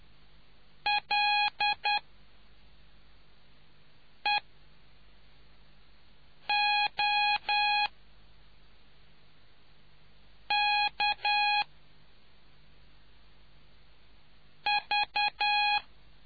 Morse code, heard as a keyed tone, about 10 wpm, spaced as for 3 wpm, 830 Hz. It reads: LEOKV